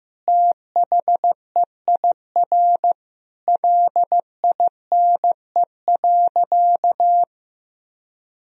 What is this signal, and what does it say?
Morse code, heard as a keyed tone, about 16 wpm, 700 Hz. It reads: THEIR LINE.
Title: Sentences from Top 1000 Words - Review Part 1 - 15wpm